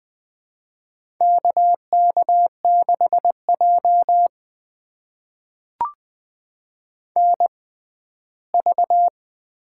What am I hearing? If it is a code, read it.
Morse code, heard as a keyed tone, about 20 wpm, spaced as for 14 wpm, 700 Hz. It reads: KK6J NV
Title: USA Call Signs - 6x Character Spacing - 20wpm